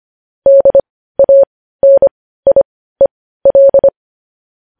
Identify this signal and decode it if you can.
Morse code, heard as a keyed tone, about 25 wpm, spaced as for 15 wpm, 550 Hz. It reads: DANIEL